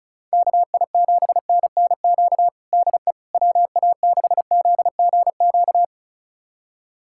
Morse code, heard as a keyed tone, about 36 wpm, 700 Hz. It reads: KI7NNQ DE WA6ZGQ
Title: USA Callsign DE Callsign - 6 Characters - 35wpm